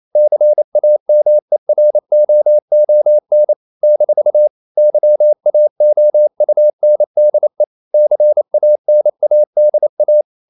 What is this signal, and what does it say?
Morse code, heard as a keyed tone, about 28 wpm, 600 Hz. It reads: CAMEROON - YAOUNDE CANADA
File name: CW / Practice / NationalCapitals28